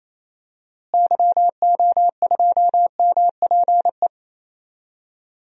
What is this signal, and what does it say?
Morse code, heard as a keyed tone, about 28 wpm, 700 Hz. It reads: YO2MPE
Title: DXCC-100 Call Signs - 6 Characters - Repeated 2x - 28wpm